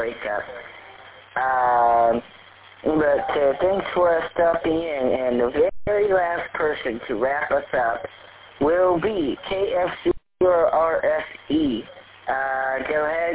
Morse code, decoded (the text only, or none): none